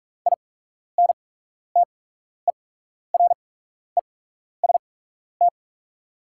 Morse code, heard as a keyed tone, about 44 wpm, spaced as for 12 wpm, 700 Hz. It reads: INTEREST
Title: 8x Character Spacing - Top 500 Words - 45wpm